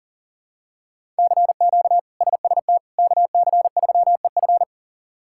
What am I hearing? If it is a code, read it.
Morse code, heard as a keyed tone, about 40 wpm, 700 Hz. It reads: CQ SST KC3EF
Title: K1USN SST QSO Practice - 40wpm